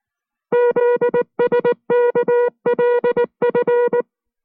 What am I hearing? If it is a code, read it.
Morse code, heard as a keyed tone, about 19 wpm, 480 Hz. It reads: ZSKLF